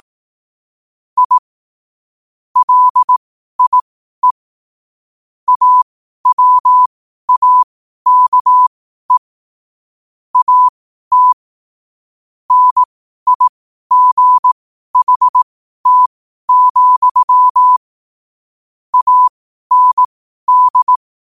Morse code, heard as a keyed tone, about 18 wpm, spaced as for 12 wpm, 1000 Hz. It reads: I LIE AWAKE AT NIGHT, AND
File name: qotdfiles / 12wpm